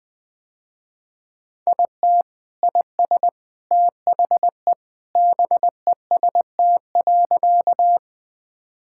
Morse code, heard as a keyed tone, about 21 wpm, 700 Hz. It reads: IT IS THE BEST.
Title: Sentences with 3-Word Phrases - Spoken Beginning - 20wpm